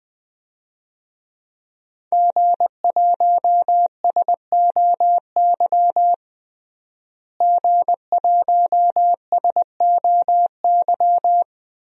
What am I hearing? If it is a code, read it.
Morse code, heard as a keyed tone, about 21 wpm, 700 Hz. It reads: G1SOY G1SOY
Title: DXCC-100 Call Signs - 5 Characters - Repeated 2x - 20wpm